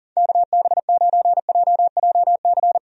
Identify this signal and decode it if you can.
Morse code, heard as a keyed tone, about 39 wpm, 700 Hz. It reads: KB9JJC
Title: USA Call Signs - 6 Characters - 40wpm